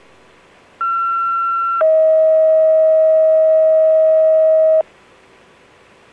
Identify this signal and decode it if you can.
Two-tone page, heard as a one-second tone, then a three-second tone, about 1360 then 640 Hz. It is heard from 0.8 to 4.8 s.